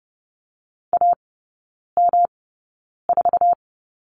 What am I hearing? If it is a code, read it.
Morse code, heard as a keyed tone, about 30 wpm, spaced as for 9 wpm, 700 Hz. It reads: AM4
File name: Lesson 18e - Letter m - Three Characters 30wpm